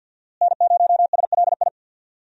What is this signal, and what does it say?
Morse code, heard as a keyed tone, about 50 wpm, 700 Hz. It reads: N0SLI
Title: USA Call Signs - 5 Characters - Repeated 2x - 50wpm